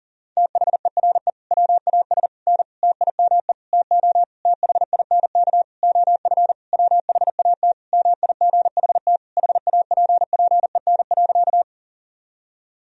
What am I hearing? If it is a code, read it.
Morse code, heard as a keyed tone, about 40 wpm, 700 Hz. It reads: THERE WAS N TIME TO THINK OF WHAT MIGHT HAPPEN.
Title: Sentences from Top 400 Words and Abbreviations 40wpm